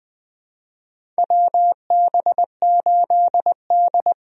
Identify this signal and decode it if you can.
Morse code, heard as a keyed tone, about 20 wpm, 700 Hz. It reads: WB8D
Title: USA Call Signs - 4 Characters - Repeated 3x - 20wpm